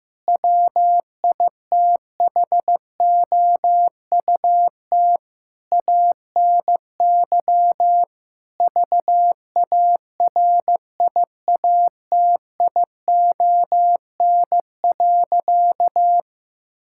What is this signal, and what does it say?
Morse code, heard as a keyed tone, about 15 wpm, 700 Hz. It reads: WITHOUT ANY VARIATION.
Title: Sentences from Top 1600 Words 15wpm